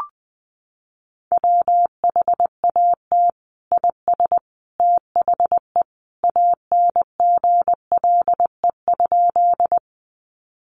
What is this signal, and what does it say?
Morse code, heard as a keyed tone, about 20 wpm, 700 Hz. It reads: WHAT IS THE ANGLE?
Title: Sentences from Top 1500 Words 20wpm